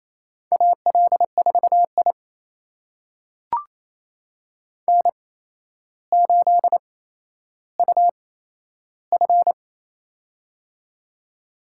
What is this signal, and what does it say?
Morse code, heard as a keyed tone, about 28 wpm, spaced as for 12 wpm, 700 Hz. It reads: AL4I N8UF
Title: USA Call Signs - 4 Characters - 8x Character Spacing - 28wpm